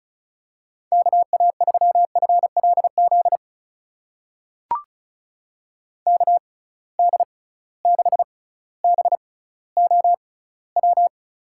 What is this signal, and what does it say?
Morse code, heard as a keyed tone, about 34 wpm, spaced as for 18 wpm, 700 Hz. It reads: KA3FLZ KD6BOW
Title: USA Call Signs - 6 Characters - 6x Character Spacing - 35wpm